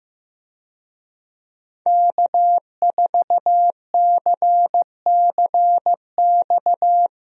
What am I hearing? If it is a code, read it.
Morse code, heard as a keyed tone, about 15 wpm, 700 Hz. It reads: K4CCX
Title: USA Call Signs - 5 Characters - Repeated 3x - 15wpm